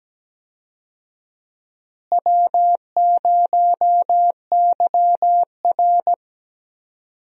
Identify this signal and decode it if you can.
Morse code, heard as a keyed tone, about 17 wpm, 700 Hz. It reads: W0YR
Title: USA Call Signs - 4 Characters - Repeated 2x - 17wpm